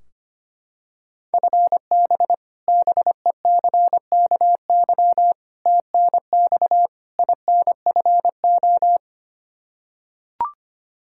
Morse code, heard as a keyed tone, about 25 wpm, 700 Hz. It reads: FB BECKY TNX INFO
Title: Practice QSO - Exchange 3 - More Info - 25wpm